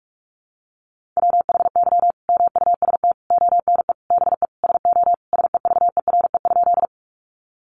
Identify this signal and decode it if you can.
Morse code, heard as a keyed tone, about 44 wpm, 700 Hz. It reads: WHY MUST ONE BE SO SEVERE?
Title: Sentences from Top 1400 Words 45wpm